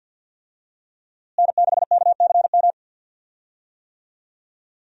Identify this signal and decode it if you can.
Morse code, heard as a keyed tone, about 52 wpm, 700 Hz. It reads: N6KCM
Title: USA Call Signs - 5 Characters - Repeated 3x - 50wpm